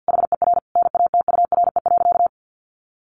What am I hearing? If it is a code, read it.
Morse code, heard as a keyed tone, about 50 wpm, 700 Hz. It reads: HER NATURE.
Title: Sentences from Top 1300 Words 50wpm